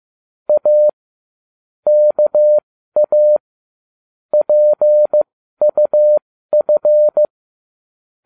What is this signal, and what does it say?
Morse code, heard as a keyed tone, about 15 wpm, spaced as for 12 wpm, 600 Hz. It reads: A KA PUF